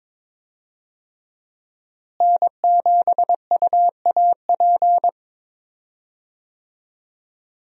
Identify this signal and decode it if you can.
Morse code, heard as a keyed tone, about 22 wpm, 700 Hz. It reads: N7UAP
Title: USA Call Signs - 5 Characters - Repeated 3x - 22wpm